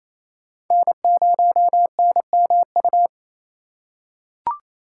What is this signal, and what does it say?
Morse code, heard as a keyed tone, about 27 wpm, 700 Hz. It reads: N0NMU
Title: USA Call Signs - 5 Characters - 4x Character Spacing - 28wpm